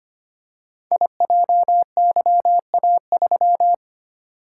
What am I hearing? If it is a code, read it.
Morse code, heard as a keyed tone, about 25 wpm, 700 Hz. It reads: IJYA3
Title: Code Groups - 5 Letter:Numbers - 25wpm